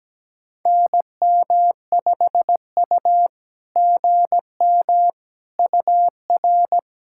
Morse code, heard as a keyed tone, about 17 wpm, 690 Hz. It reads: NM5U GM UR
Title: SOTA QSO Practice 17wpm